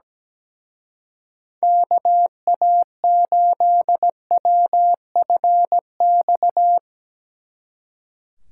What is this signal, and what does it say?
Morse code, heard as a keyed tone, about 17 wpm, 700 Hz. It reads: KA8WFX
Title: Speed Racing - USA 2x3 Call Signs 17wpm